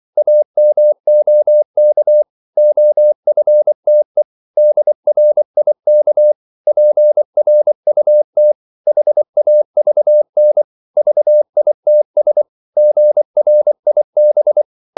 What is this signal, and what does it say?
Morse code, heard as a keyed tone, about 24 wpm, 600 Hz. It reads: AMOK OFTE DRIK PRUT HAVN VITS GRIB
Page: Korte danske ord 24wpm